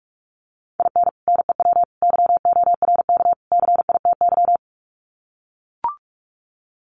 Morse code, heard as a keyed tone, about 44 wpm, 700 Hz. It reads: IN NEW YORK CITY
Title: Sets of 4 Words 45wpm